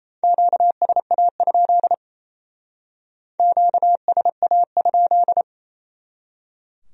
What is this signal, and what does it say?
Morse code, heard as a keyed tone, about 37 wpm, 700 Hz. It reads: QSA? QSA?